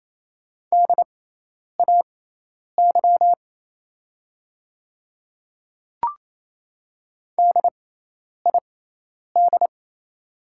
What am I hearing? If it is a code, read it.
Morse code, heard as a keyed tone, about 28 wpm, spaced as for 10 wpm, 700 Hz. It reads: DAY DID
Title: 6x Character Spacing - Top 200 Words - 28wpm